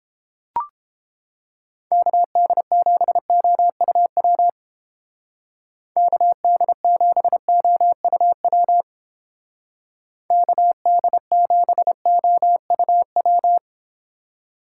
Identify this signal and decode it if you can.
Morse code, heard as a keyed tone, about 31 wpm, 700 Hz. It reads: KD7OUW KD7OUW KD7OUW